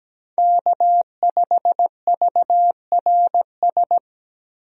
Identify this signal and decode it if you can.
Morse code, heard as a keyed tone, about 17 wpm, 700 Hz. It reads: K5VRS
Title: USA Call Signs - 5 Characters - 17wpm